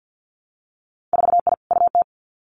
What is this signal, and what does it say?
Morse code, heard as a keyed tone, about 50 wpm, 700 Hz. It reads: 4I UT
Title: Winter Field Day Exchanges 50wpm